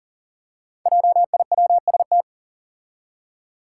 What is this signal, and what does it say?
Morse code, heard as a keyed tone, about 40 wpm, 700 Hz. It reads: JIWST
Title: Five Challenging Letter-Number Combinations 40wpm